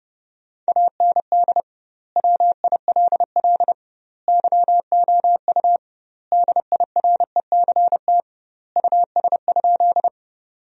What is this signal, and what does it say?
Morse code, heard as a keyed tone, about 30 wpm, 700 Hz. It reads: AND WILL YOU DIRECT US?
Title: 2x Word-Spacing - Sentences from Top 400 Words - 30wpm